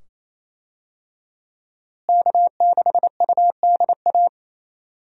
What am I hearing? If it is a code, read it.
Morse code, heard as a keyed tone, about 28 wpm, 700 Hz. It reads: K6UDA
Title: USA Call Signs - 5 Characters - Repeated 2x - 28wpm